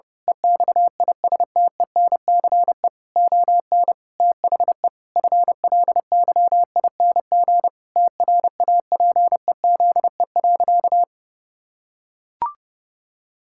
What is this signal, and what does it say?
Morse code, heard as a keyed tone, about 28 wpm, 700 Hz. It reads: EXISTENCE ON THE FLYING TRAPEZE.
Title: Pangram Challenge v2 - 30wpm